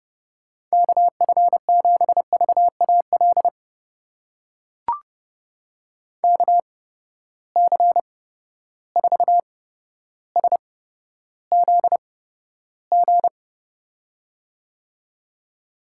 Morse code, heard as a keyed tone, about 30 wpm, spaced as for 13 wpm, 700 Hz. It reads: KF7VAL KC4SZG
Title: USA Call Signs - 6 Characters - 8x Character Spacing - 30wpm